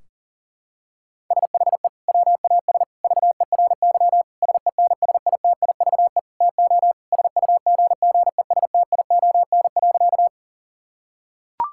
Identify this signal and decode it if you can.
Morse code, heard as a keyed tone, about 37 wpm, 700 Hz. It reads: SHE WAS VERY SENSITIVE TO SUGGESTION.
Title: Sentences from Top 1600 Words 40wpm